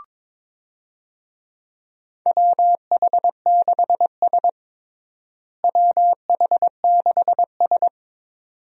Morse code, heard as a keyed tone, about 22 wpm, 700 Hz. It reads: WH6S WH6S
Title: USA Call Signs - 4 Characters - Repeated 2x - 22wpm